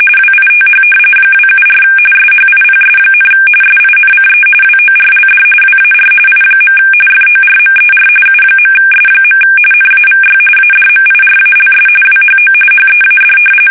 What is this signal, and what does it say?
RTTY, about 45.45 baud. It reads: WITH 274 SHIPS, SINCE WORLD WAR I. FORMER U.S. NAVY ADMIRALS WHO HEAD
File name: fsk850 / usnavy1